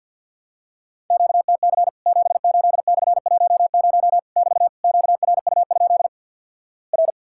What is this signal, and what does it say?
Morse code, heard as a keyed tone, about 48 wpm, 700 Hz. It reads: QTC 78/10 <BT> QRU? R
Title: Worked All Europe DX Contest - QTC Exchange with Repeats - 50wpm